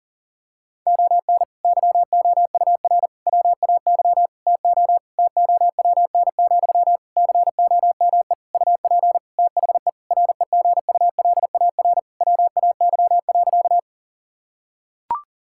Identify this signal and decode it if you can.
Morse code, heard as a keyed tone, about 40 wpm, 700 Hz. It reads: ON YOUR WAY TO TOWN, COME UP THE REGULAR WAY.
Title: Sentences from Top 1000 Words - Review Part 6 - 40wpm